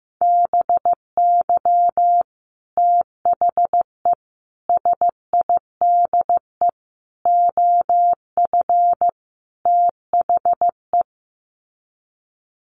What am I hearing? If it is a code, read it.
Morse code, heard as a keyed tone, about 15 wpm, 700 Hz. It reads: BY THE SIDE OF THE
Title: Sets of 5 Words Encore2 15wpm